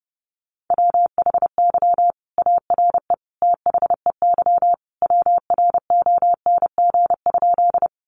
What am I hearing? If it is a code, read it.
Morse code, heard as a keyed tone, about 30 wpm, 700 Hz. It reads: WHY ARE THEY WRONG?
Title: Sentences from Top 900 Words 30wpm